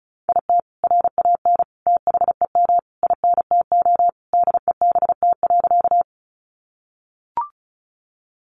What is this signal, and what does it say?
Morse code, heard as a keyed tone, about 34 wpm, 700 Hz. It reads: IT RAN THEM INTO DEBT.
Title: Sentences from Top 1300 Words 35wpm